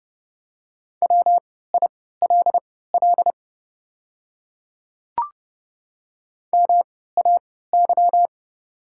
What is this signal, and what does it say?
Morse code, heard as a keyed tone, about 30 wpm, spaced as for 17 wpm, 690 Hz. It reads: WILL MAY
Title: 3x Character Spacing - Top 100 Words - 30wpm